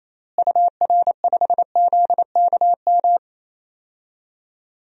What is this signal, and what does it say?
Morse code, heard as a keyed tone, about 28 wpm, 700 Hz. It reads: UR5ZKM
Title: DXCC-100 Call Signs - 6 Characters - Repeated 3x - 28wpm